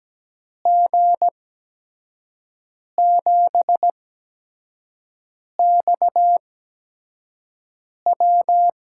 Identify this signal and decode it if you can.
Morse code, heard as a keyed tone, about 17 wpm, spaced as for 5 wpm, 700 Hz. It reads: G7XW
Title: USA Call Signs - 5 Characters - 8x Character Spacing - 17wpm